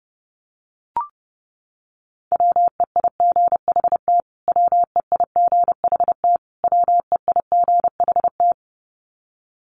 Morse code, heard as a keyed tone, about 30 wpm, 700 Hz. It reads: WEIGHT WEIGHT WEIGHT
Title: Top 500 Words Repeated 30wpm